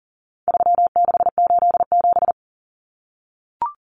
Morse code, heard as a keyed tone, about 40 wpm, 700 Hz. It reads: 3687
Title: Numbers - 4-Digit 40wpm